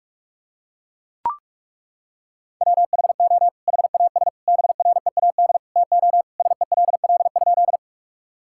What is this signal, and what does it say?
Morse code, heard as a keyed tone, about 41 wpm, 700 Hz. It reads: WHO HAS BREAD TO SELL?
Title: Sentences from Top 1600 Words 45wpm